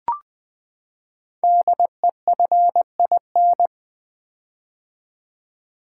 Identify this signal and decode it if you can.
Morse code, heard as a keyed tone, about 20 wpm, 700 Hz. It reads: DEFIN
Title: Build A Word - 5 Letters At A Time - 10-Letter Words - 20wpm